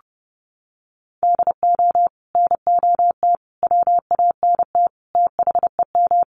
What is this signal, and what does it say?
Morse code, heard as a keyed tone, about 30 wpm, 700 Hz. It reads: DO NOT WANT THEM
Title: Sets of 5 Words Encore 30wpm